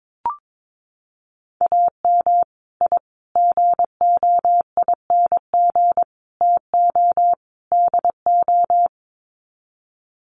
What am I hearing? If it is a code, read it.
Morse code, heard as a keyed tone, about 22 wpm, 700 Hz. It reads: AM I GOING TO DO
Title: Sets of 5 Words Encore 22wpm